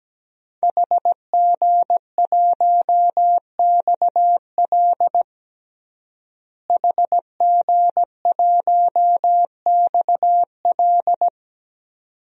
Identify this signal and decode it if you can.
Morse code, heard as a keyed tone, about 17 wpm, 700 Hz. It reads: HG1XL HG1XL
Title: DXCC-100 Call Signs - 5 Characters - Repeated 2x - 17wpm